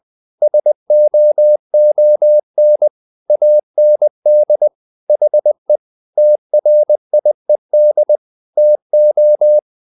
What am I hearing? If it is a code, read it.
Morse code, heard as a keyed tone, about 20 wpm, 600 Hz. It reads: SOON AND HE TRIED TO
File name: the old man and the sea / the old man and the sea part 1 20wpm